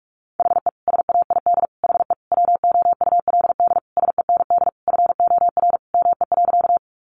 Morse code, heard as a keyed tone, about 46 wpm, 700 Hz. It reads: HE SAID HE WOULD SEND FOR ME.
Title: Sentences from Top 1500 Words 45wpm